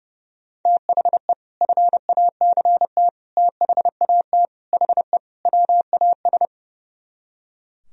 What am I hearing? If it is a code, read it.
Morse code, heard as a keyed tone, about 37 wpm, 700 Hz. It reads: THE FACT THAT HE WAS
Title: Speed-Racing - Sets of 5 Words 30wpm